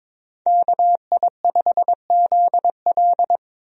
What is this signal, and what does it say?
Morse code, heard as a keyed tone, about 21 wpm, 700 Hz. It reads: KI5ZL
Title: USA Call Signs - 5 Characters - Repeated 2x - 22wpm